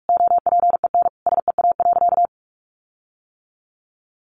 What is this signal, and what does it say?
Morse code, heard as a keyed tone, about 46 wpm, 700 Hz. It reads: OPEN SEA.
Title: Sentences from Top 1500 Words 45wpm